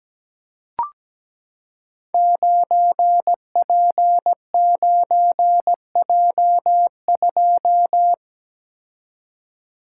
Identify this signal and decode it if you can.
Morse code, heard as a keyed tone, about 17 wpm, 700 Hz. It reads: 9P9J2
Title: Code Groups - 5 Letter:Numbers - 17wpm